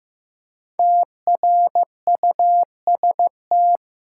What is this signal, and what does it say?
Morse code, heard as a keyed tone, about 15 wpm, 700 Hz. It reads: TRUST
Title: Five Letter Words - 15wpm